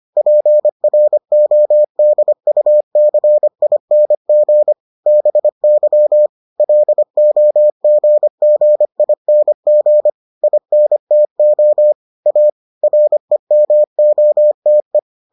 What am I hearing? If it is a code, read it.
Morse code, heard as a keyed tone, about 25 wpm, 600 Hz. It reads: PRODUCING BY LOGGING INTO A REMOTE